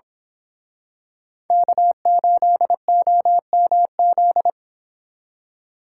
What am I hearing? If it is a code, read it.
Morse code, heard as a keyed tone, about 26 wpm, 700 Hz. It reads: K8OMZ
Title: Speed Racing - USA 1x3 Call Signs 22wpm